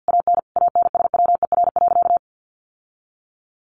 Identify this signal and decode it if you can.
Morse code, heard as a keyed tone, about 50 wpm, 700 Hz. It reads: AN ANSWER.
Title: Sentences from Top 1200 Words 50wpm